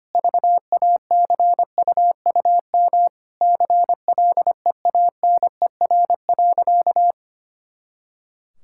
Morse code, heard as a keyed tone, about 25 wpm, 700 Hz. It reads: VACUUM CLEANER.
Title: Pangram Challenge v2 - 25wpm